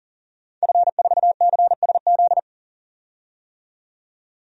Morse code, heard as a keyed tone, about 40 wpm, 700 Hz. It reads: F4CSZ
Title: DXCC-100 Call Signs - 5 Characters - 40wpm